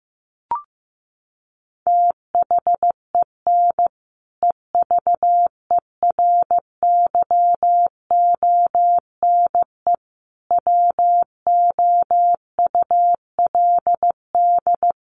Morse code, heard as a keyed tone, about 15 wpm, 700 Hz. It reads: THEN EVERYONE WOULD
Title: Sentences from Top 1400 Words 15wpm